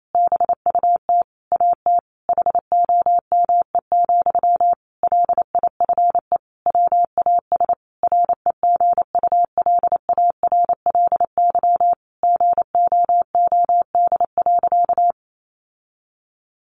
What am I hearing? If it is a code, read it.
Morse code, heard as a keyed tone, about 28 wpm, 700 Hz. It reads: BUT AT HOME, LIFE WAS REGULARLY GOOD.
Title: Sentences from Top 1400 Words 28wpm